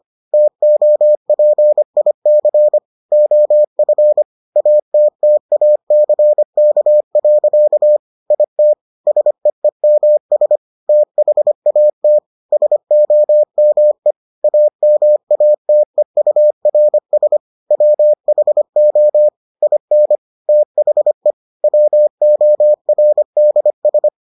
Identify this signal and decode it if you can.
Morse code, heard as a keyed tone, about 25 wpm, 600 Hz. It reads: TOPIC OF ATTACK. IT SEEMS THAT SOME AMATEURS WHO IN THE WORDS